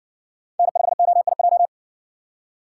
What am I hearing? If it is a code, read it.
Morse code, heard as a keyed tone, about 60 wpm, 700 Hz. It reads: N5OIJ